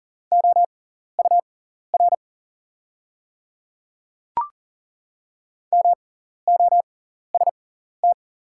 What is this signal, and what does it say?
Morse code, heard as a keyed tone, about 41 wpm, spaced as for 14 wpm, 700 Hz. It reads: OUR MOST
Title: 6x Character Spacing - Top 200 Words - 40wpm